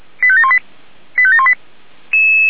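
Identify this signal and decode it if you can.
Five-tone selective call (ZVEI), heard as quick runs of five tones, tones of about 70 ms. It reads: 85718 85718
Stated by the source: ZVEI five-tone selective call received on an SDR.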